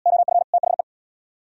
Morse code, heard as a keyed tone, about 69 wpm, 700 Hz. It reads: OF THE